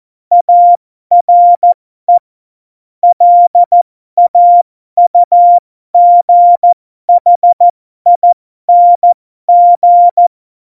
Morse code, heard as a keyed tone, about 14 wpm, spaced as for 12 wpm, 700 Hz. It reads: ARE LAUGHING
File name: Morsekurs / Witze